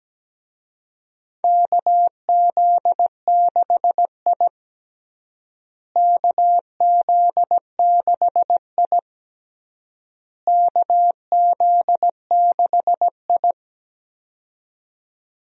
Morse code, heard as a keyed tone, about 17 wpm, 700 Hz. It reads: KZ6I KZ6I KZ6I
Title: USA Call Signs - 4 Characters - Repeated 3x - 17wpm